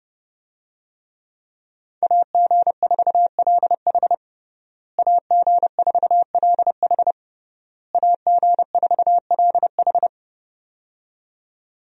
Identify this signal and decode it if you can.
Morse code, heard as a keyed tone, about 32 wpm, 700 Hz. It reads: AG4LH AG4LH AG4LH
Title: USA Call Signs - 5 Characters - Repeated 3x - 30wpm